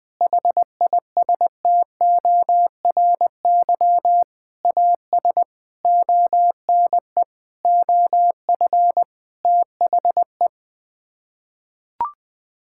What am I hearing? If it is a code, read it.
Morse code, heard as a keyed tone, about 20 wpm, 700 Hz. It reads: HISTORY AS ONE OF THE
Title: Sets of 8 Words - 20wpm